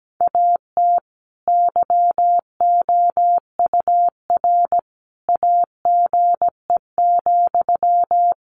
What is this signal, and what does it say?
Morse code, heard as a keyed tone, about 17 wpm, 700 Hz. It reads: AT YOUR AGE,
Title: Sentences from Top 1500 Words 17wpm